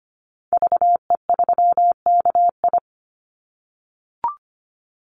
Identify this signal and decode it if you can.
Morse code, heard as a keyed tone, about 25 wpm, 700 Hz. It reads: VE3KI
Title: Call Signs - 25wpm